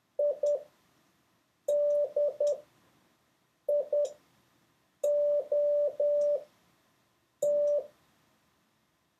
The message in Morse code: IDIOT